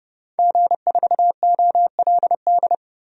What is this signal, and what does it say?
Morse code, heard as a keyed tone, about 32 wpm, 700 Hz. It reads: G4OLD
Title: DXCC-100 Call Signs - 5 Characters - 30wpm